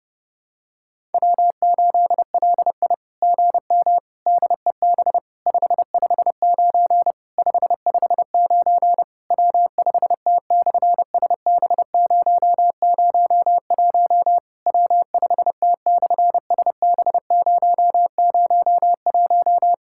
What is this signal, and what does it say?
Morse code, heard as a keyed tone, about 30 wpm, 700 Hz. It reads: W8LI GM DEB 559 559 W5T/SB001 W5T/SB001
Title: SOTA QSO Practice 30wpm